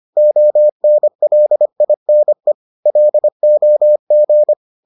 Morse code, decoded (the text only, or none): ONLINE LOG